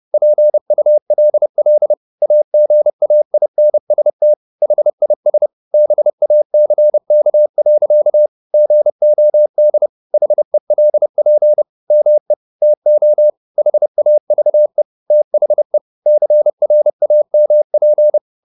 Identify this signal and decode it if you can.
Morse code, heard as a keyed tone, about 30 wpm, 600 Hz. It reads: PULL AGAINST HIS BACK. GOD HELP ME TO HAVE THE CRAMP
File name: the old man and the sea / the old man and the sea 1 30wpm